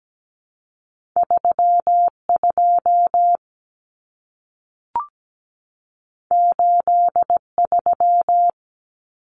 Morse code, heard as a keyed tone, about 17 wpm, 700 Hz. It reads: 32 83
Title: Common Cut Numbers - 2-Digit 17wpm